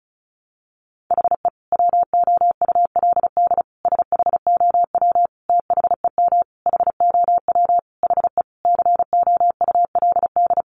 Morse code, heard as a keyed tone, about 34 wpm, 700 Hz. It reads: HE WOULD SHOW THEM HOW HE COULD
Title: Sentences from Top 1300 Words 35wpm